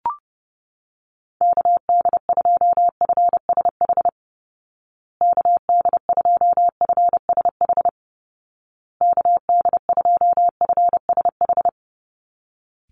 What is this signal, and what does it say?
Morse code, heard as a keyed tone, about 30 wpm, 700 Hz. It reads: KD2FSH KD2FSH KD2FSH
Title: Call Signs Repeated 2x - 30wpm